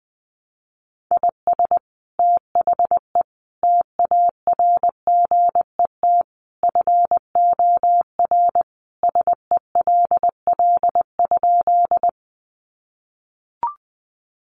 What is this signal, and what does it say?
Morse code, heard as a keyed tone, about 20 wpm, 700 Hz. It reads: IS THE TARGET FOR SELL?
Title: Sentences from Top 1200 Words 20wpm